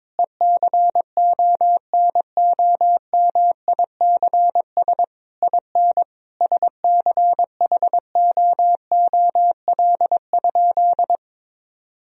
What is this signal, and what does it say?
Morse code, heard as a keyed tone, about 21 wpm, 700 Hz. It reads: ECONOMICS IN SCHOOL?
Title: Sentences from Top 1000 Words - Review Part 3 - 22wpm